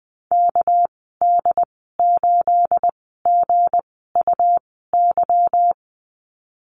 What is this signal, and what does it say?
Morse code, heard as a keyed tone, about 20 wpm, spaced as for 14 wpm, 700 Hz. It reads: KD8GUY
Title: 245 Lesson 39g - Letter z - Call Signs 20wpm